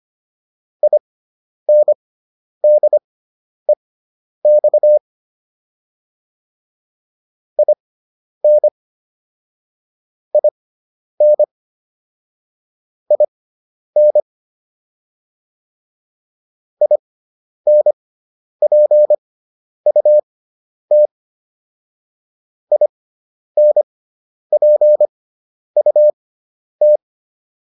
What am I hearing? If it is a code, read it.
Morse code, heard as a keyed tone, about 25 wpm, spaced as for 10 wpm, 600 Hz. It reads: INDEX IN IN IN INPUT INPUT